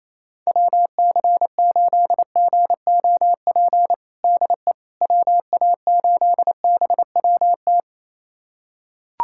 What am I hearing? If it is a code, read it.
Morse code, heard as a keyed tone, about 28 wpm, 700 Hz. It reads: WC8GOP DE WA8BWT
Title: USA Callsign DE Callsign 28wpm